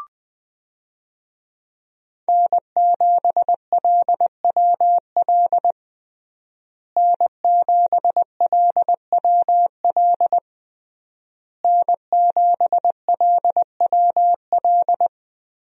Morse code, heard as a keyed tone, about 20 wpm, 700 Hz. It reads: N7LWL N7LWL N7LWL
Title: USA Call Signs - 5 Characters - Repeated 3x - 20wpm